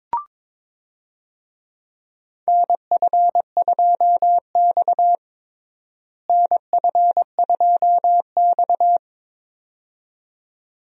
Morse code, heard as a keyed tone, about 22 wpm, 700 Hz. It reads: NF2X NF2X
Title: USA Call Signs - 4 Characters - Repeated 2x - 22wpm